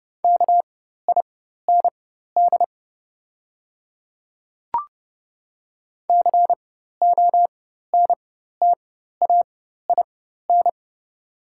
Morse code, heard as a keyed tone, about 29 wpm, spaced as for 14 wpm, 700 Hz. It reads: KIND CONTAIN
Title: 4x Character Spacing - Top 500 Words - 30wpm